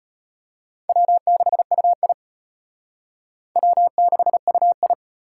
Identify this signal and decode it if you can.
Morse code, heard as a keyed tone, about 33 wpm, 700 Hz. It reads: W6UI W6UI